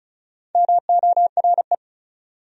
Morse code, heard as a keyed tone, about 36 wpm, 700 Hz. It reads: MORE
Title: Top 100 Words 35wpm